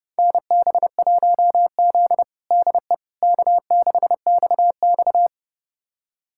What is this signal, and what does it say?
Morse code, heard as a keyed tone, about 30 wpm, 700 Hz. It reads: NB1Z DE K6XX
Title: USA Callsign DE Callsign - 4 Characters - 30wpm